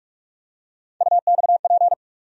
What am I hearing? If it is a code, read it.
Morse code, heard as a keyed tone, about 43 wpm, 700 Hz. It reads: UXP
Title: USA Call Sign Postfixes - 3 Letters 45wpm